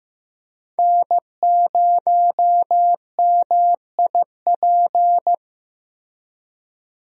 Morse code, heard as a keyed tone, about 16 wpm, 700 Hz. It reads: N0MIP
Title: USA Call Signs - 5 Characters - Repeated 2x - 15wpm